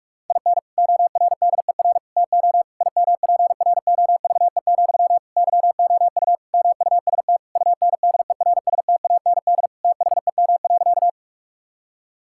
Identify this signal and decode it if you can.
Morse code, heard as a keyed tone, about 46 wpm, 700 Hz. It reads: IN ORDER TO IMPROVE, YOU MUST UNDERSTAND THEM.
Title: Sentences from Top 1000 Words - Review Part 5 - 45wpm